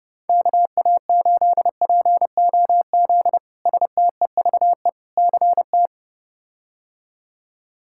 Morse code, heard as a keyed tone, about 30 wpm, 700 Hz. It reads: KA8POZ STEVE CT
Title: K1USN SST QSO Practice - 30wpm